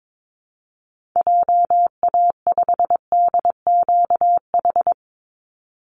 Morse code, heard as a keyed tone, about 22 wpm, 690 Hz. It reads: JA5DQH
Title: Call Signs Repeated 3x - 22wpm